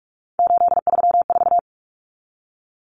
Morse code, heard as a keyed tone, about 44 wpm, 700 Hz. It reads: 834
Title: Numbers - 3-Digit 45wpm